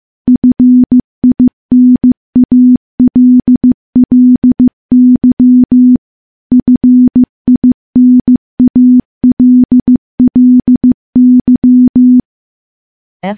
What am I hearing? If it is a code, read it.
Morse code, heard as a keyed tone, about 15 wpm, 250 Hz. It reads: FINALLY FINALLY
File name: morse / morse-training-15WPM